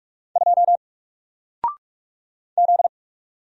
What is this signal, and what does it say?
Morse code, heard as a keyed tone, about 46 wpm, 700 Hz. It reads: 2 Z
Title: Single Letter-Number Alt 45wpm